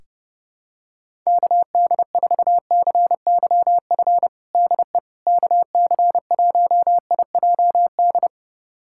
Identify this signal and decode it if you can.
Morse code, heard as a keyed tone, about 31 wpm, 700 Hz. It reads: KD4CYF DE KC1IJD
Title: USA Callsign DE Callsign - 6 Characters - 30wpm